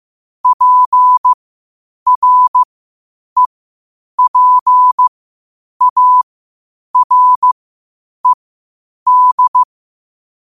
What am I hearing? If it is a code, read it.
Morse code, heard as a keyed tone, about 15 wpm, spaced as for 8 wpm, 1000 Hz. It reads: PREPARED
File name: qotdfiles / 8wpm